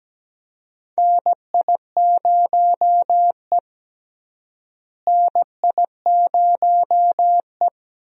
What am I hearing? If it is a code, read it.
Morse code, heard as a keyed tone, about 17 wpm, 700 Hz. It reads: NI0E NI0E
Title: USA Call Signs - 4 Characters - Repeated 2x - 17wpm